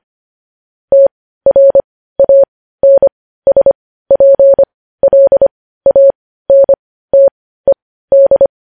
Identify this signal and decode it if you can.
Morse code, heard as a keyed tone, about 25 wpm, spaced as for 15 wpm, 550 Hz. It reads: TRANSPLANTED
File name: podcast / News headlines-1x-25x15